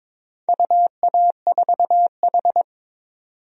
Morse code, heard as a keyed tone, about 22 wpm, 700 Hz. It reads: UA4H
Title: DXCC-100 Call Signs - 4 Characters - 22wpm